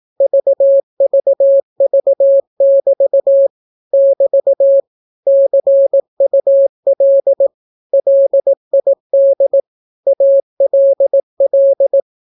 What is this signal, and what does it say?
Morse code, heard as a keyed tone, about 18 wpm, 550 Hz. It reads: VVV<BT> <BT> CUL LID ALL
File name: naukapodstaw / skroty / skroty